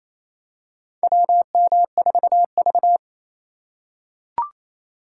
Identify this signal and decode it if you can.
Morse code, heard as a keyed tone, about 28 wpm, 700 Hz. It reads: WM4V
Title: USA Call Signs - 4 Characters - 3x Character Spacing - 28wpm